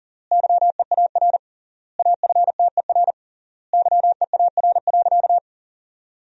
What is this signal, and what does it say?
Morse code, heard as a keyed tone, about 37 wpm, 700 Hz. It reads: YEAR AFTER YEAR.
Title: 3x Word-Spacing - Sentences from Top 300 Words - 40wpm